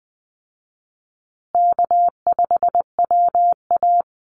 Morse code, heard as a keyed tone, about 20 wpm, 700 Hz. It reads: K5WA
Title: Easy Call Signs Repeated 3x - 20wpm